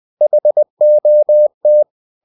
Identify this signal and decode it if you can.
Morse code, heard as a keyed tone, about 20 wpm, 600 Hz. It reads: HOT
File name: the old man and the sea part 2 20wpm